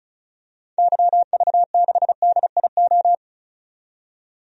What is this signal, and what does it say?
Morse code, heard as a keyed tone, about 34 wpm, 700 Hz. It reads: YV6DIO
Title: DXCC-100 Call Signs - 6 Characters - 35wpm